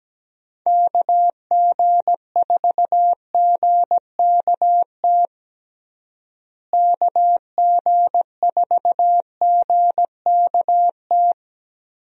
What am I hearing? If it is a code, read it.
Morse code, heard as a keyed tone, about 17 wpm, 700 Hz. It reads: KG4GKT KG4GKT
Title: USA Call Signs - 6 Characters - Repeated 2x - 17wpm